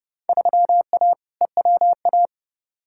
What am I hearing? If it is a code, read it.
Morse code, heard as a keyed tone, about 32 wpm, 700 Hz. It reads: 3A EWA
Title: Field Day Exchanges Encore - 30wpm